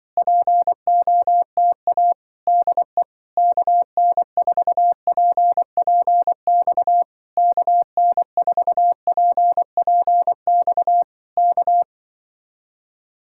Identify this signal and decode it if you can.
Morse code, heard as a keyed tone, about 23 wpm, 700 Hz. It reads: POTA DE KN4PPX KN4PPX K
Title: Speed-Racing Common QSO Phrases - 20wpm